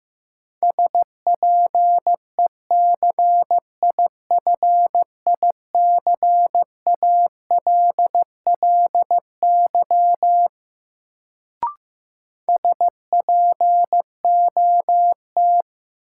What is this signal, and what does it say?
Morse code, heard as a keyed tone, about 15 wpm, 700 Hz. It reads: SPECIFICALLY SPOT
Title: Sentences from Top 1100 Words - New Word Drill Alt - 15wpm